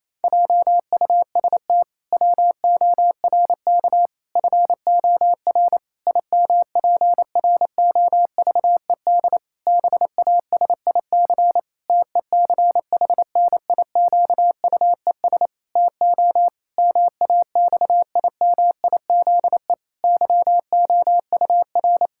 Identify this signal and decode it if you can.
Morse code, heard as a keyed tone, about 28 wpm, 700 Hz. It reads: JUST WORK FOR IMPROVED BASIC TECHNIQUES TO MAXIMIZE YOUR
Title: Pangram Challenge v2 - 28wpm